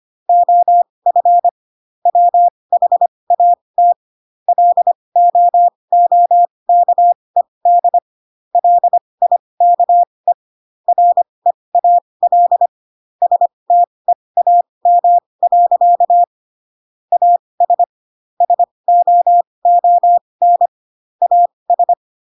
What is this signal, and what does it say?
Morse code, heard as a keyed tone, about 25 wpm, spaced as for 20 wpm, 700 Hz. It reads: OF WHAT LOOKED LIKE REAL STEAM. AS SOON AS